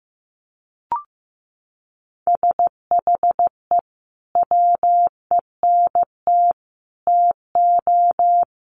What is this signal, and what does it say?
Morse code, heard as a keyed tone, about 15 wpm, 700 Hz. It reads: SHE WENT TO
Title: Sentences from Top 900 Words 15wpm